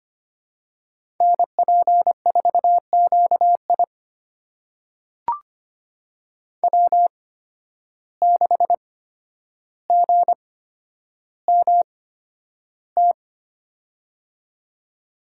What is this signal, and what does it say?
Morse code, heard as a keyed tone, about 25 wpm, spaced as for 11 wpm, 700 Hz. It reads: NP4QI W6GMT
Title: USA Call Signs - 5 Characters - 8x Character Spacing - 25wpm